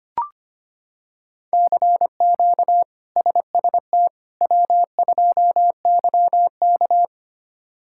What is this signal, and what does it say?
Morse code, heard as a keyed tone, about 25 wpm, 700 Hz. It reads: CQ SST W2YK
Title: K1USN SST QSO Practice - 25wpm